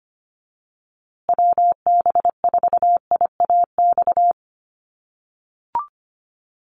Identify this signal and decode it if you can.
Morse code, heard as a keyed tone, about 25 wpm, 700 Hz. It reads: WB4IAX
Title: USA Call Signs Repeated 3x 25wpm